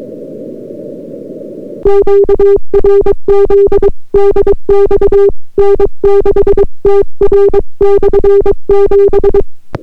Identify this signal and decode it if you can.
Morse code, heard as a keyed tone, about 22 wpm, 400 Hz. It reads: QRZ DX N6TR/7